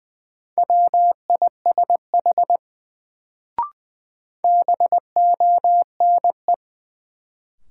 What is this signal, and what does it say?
Morse code, heard as a keyed tone, about 20 wpm, 700 Hz. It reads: WISH BONE
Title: Sentences from Top 1600 Words - New Word Drill - 20wpm